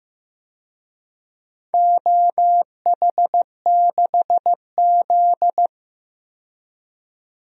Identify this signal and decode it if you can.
Morse code, heard as a keyed tone, about 15 wpm, 700 Hz. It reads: OH6Z
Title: DXCC-100 Call Signs - 4 Characters - 15wpm